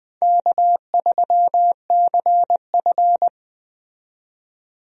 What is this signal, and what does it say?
Morse code, heard as a keyed tone, about 20 wpm, 700 Hz. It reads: K3CF
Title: USA Call Signs - 4 Characters - 20wpm